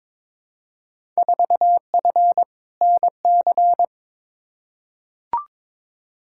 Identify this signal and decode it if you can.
Morse code, heard as a keyed tone, about 21 wpm, 700 Hz. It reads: 4F NC
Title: Field Day Exchanges - 22wpm